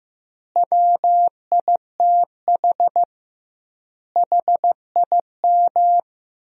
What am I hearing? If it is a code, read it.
Morse code, heard as a keyed tone, about 15 wpm, 700 Hz. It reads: WITH HIM
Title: 2x Word-Spacing - Sentences from Top 500 Words - 15wpm